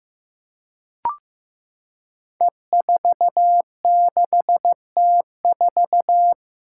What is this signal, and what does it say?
Morse code, heard as a keyed tone, about 15 wpm, 700 Hz. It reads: E46T4
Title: Five Challenging Letter-Number Combinations 15wpm